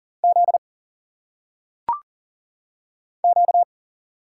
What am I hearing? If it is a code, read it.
Morse code, heard as a keyed tone, about 43 wpm, 700 Hz. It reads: Z Q
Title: Single Letters - Phonetic Pronunciation 40wpm